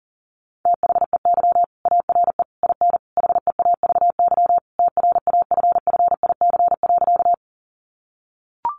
Morse code, heard as a keyed tone, about 40 wpm, 700 Hz. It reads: THEY ARE IN HEAVY TRAFFIC.
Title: Sentences from Top 1300 Words 40wpm